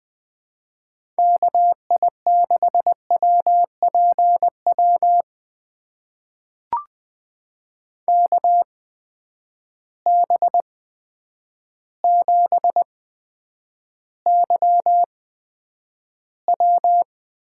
Morse code, heard as a keyed tone, about 20 wpm, spaced as for 9 wpm, 700 Hz. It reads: KI6WPW KB7YW
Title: USA Call Signs - 6 Characters - 8x Character Spacing - 20wpm